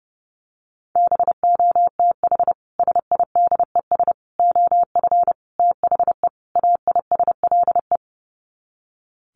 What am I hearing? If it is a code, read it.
Morse code, heard as a keyed tone, about 30 wpm, 700 Hz. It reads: BOTH SIDES OF THE AISLE
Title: Sets of 5 Words Encore2 30wpm